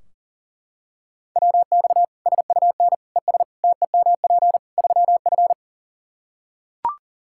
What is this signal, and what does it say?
Morse code, heard as a keyed tone, about 40 wpm, 700 Hz. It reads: WX SUN ES TEMP 3F
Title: Practice QSO - Exchange 2 - Info - 40wpm